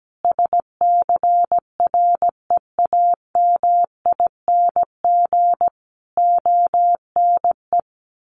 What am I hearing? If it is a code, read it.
Morse code, heard as a keyed tone, about 17 wpm, 700 Hz. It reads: SCREAMING ONE
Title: A Sound Of Thunder 17wpm